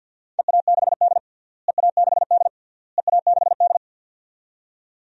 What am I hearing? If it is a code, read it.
Morse code, heard as a keyed tone, about 48 wpm, 700 Hz. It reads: EA6D EA6D EA6D
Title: DXCC-100 Call Signs - 4 Characters - Repeated 3x - 50wpm